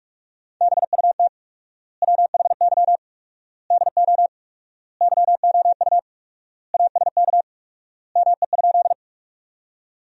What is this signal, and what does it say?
Morse code, heard as a keyed tone, about 46 wpm, 700 Hz. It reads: BUT WHY DO YOU ASK ME?
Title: 4x Word-Spacing - Sentences from Top 200 Words - 45wpm